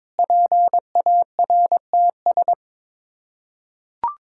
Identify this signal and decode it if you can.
Morse code, heard as a keyed tone, about 22 wpm, 700 Hz. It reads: PARTS
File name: practice / 22wpm / Sets of 7 Words - 22wpm